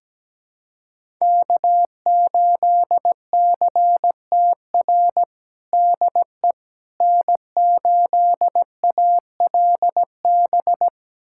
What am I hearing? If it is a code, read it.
Morse code, heard as a keyed tone, about 17 wpm, 700 Hz. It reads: K8CTR DE N8ALB
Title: USA Callsign DE Callsign - 5 Characters - 17wpm